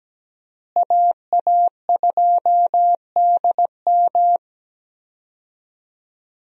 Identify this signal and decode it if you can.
Morse code, heard as a keyed tone, about 17 wpm, 700 Hz. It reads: AA2DM
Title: USA Call Signs - 5 Characters - Repeated 2x - 17wpm